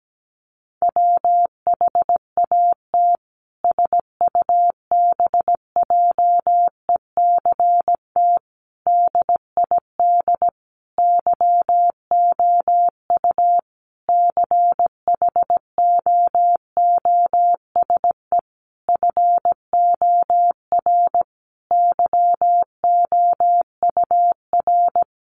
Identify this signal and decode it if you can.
Morse code, heard as a keyed tone, about 17 wpm, 700 Hz. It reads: WHAT SUBJECT DID YOU CHOOSE FOR YOUR
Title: Sentences from Top 1500 Words 17wpm